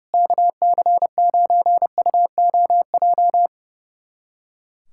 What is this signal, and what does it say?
Morse code, heard as a keyed tone, about 30 wpm, 700 Hz. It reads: KC9UOJ
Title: USA Callsign DE Callsign - 6 Characters - 30wpm